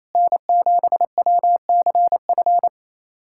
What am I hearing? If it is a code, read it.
Morse code, heard as a keyed tone, about 28 wpm, 700 Hz. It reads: N7WCF
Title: USA Callsign DE Callsign - 5 Characters - 28wpm